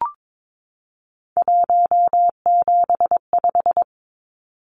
Morse code, heard as a keyed tone, about 22 wpm, 700 Hz. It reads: 175
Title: Numbers - 3-Digit 22wpm